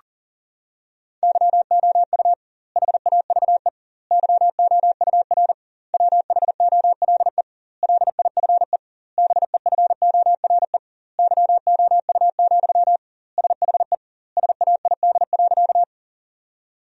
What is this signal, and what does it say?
Morse code, heard as a keyed tone, about 40 wpm, 700 Hz. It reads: YOU HAVE YOUR WHOLE LIFE BEFORE YOU, SHE SAID.
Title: 2x Word-Spacing - Sentences from Top 400 Words - 40wpm